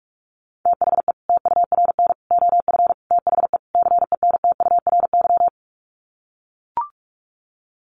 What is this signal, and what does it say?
Morse code, heard as a keyed tone, about 44 wpm, 700 Hz. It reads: THE TURN OF THE CENTURY
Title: Sets of 5 Words 45wpm